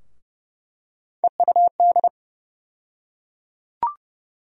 Morse code, heard as a keyed tone, about 32 wpm, 700 Hz. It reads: EUD
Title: USA Call Sign Postfixes - 3 Letters 30wpm